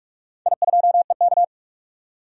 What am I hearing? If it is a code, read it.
Morse code, heard as a keyed tone, about 46 wpm, 700 Hz. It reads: I2EK
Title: DXCC-100 Call Signs - 4 Characters - 45wpm